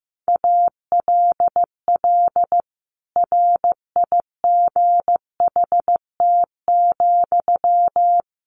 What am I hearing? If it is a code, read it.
Morse code, heard as a keyed tone, about 15 wpm, 700 Hz. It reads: ALL RIGHT,
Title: Sentences from Top 200 Words 15wpm